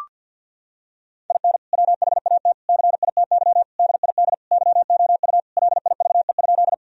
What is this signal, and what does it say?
Morse code, heard as a keyed tone, about 52 wpm, 700 Hz. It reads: IN WHAT CITY DID YOU LIVE?
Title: Sentences from Top 1000 Words - Review Part 1 - 50wpm